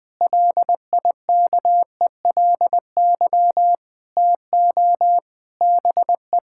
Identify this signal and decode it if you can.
Morse code, heard as a keyed tone, about 20 wpm, 700 Hz. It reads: LIKELY TO BE
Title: Sentences with 5-Word Phrases - 20wpm